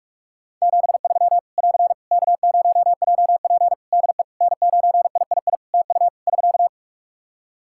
Morse code, heard as a keyed tone, about 44 wpm, 700 Hz. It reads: 73 <AR> K0JP DE N9III TU <SK>
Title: Practice QSO - Exchange 4 - Ending - 45wpm